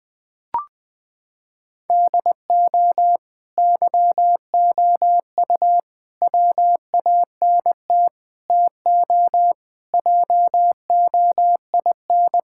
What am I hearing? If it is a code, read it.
Morse code, heard as a keyed tone, about 20 wpm, 700 Hz. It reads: DO YOU WANT TO JOIN
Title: Sentences with 4-Word Phrases - 20wpm